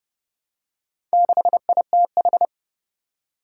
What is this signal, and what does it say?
Morse code, heard as a keyed tone, about 30 wpm, 700 Hz. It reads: 6ITH
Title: Four Challenging Letter-Number Combinations 30wpm